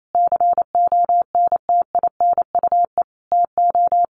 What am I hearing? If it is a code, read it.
Morse code, heard as a keyed tone, about 28 wpm, 700 Hz. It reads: CONTINUE TO
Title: Sentences from Top 1400 Words 28wpm